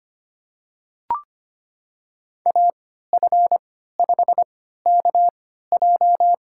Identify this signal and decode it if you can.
Morse code, heard as a keyed tone, about 25 wpm, spaced as for 14 wpm, 700 Hz. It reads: AF5KJ
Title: USA Call Signs - 5 Characters - 3x Character Spacing - 25wpm